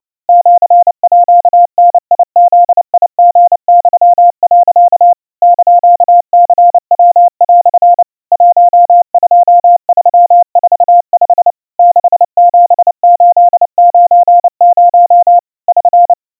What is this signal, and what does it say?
Morse code (Japanese Wabun code), heard as a keyed tone, about 29 wpm, 700 Hz. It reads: シーダブリユ、 <KN>ニヤ) 12345 67890 <SN>
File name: aiueo 100 cpm 700 Hz Crispy